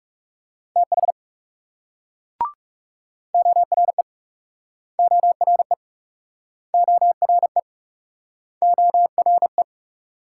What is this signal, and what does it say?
Morse code, heard as a keyed tone, about 39 wpm, 700 Hz. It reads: TH ORE ORE ORE ORE